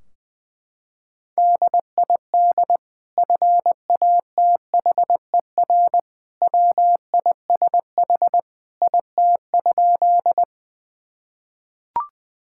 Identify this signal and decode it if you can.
Morse code, heard as a keyed tone, about 20 wpm, 700 Hz. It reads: DID FATHER WISH IT?
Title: Sentences from Top 1600 Words 20wpm